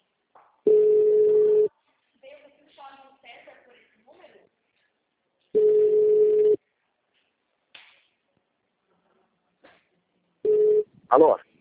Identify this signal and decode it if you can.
Ringback tone (Europe, 425 Hz), with speech.
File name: out- -2003- - - .35474